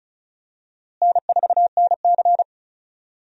Morse code, heard as a keyed tone, about 34 wpm, 700 Hz. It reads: N4NC
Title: USA Call Signs - 4 Characters - Repeated 3x - 35wpm